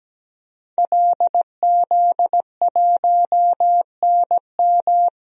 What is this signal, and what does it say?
Morse code, heard as a keyed tone, about 17 wpm, 700 Hz. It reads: LZ1NM
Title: DXCC-100 Call Signs - 5 Characters - Repeated 3x - 17wpm